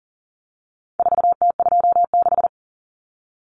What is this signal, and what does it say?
Morse code, heard as a keyed tone, about 40 wpm, 700 Hz. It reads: 4T26
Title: Common Cut Numbers - 4-Digit 40wpm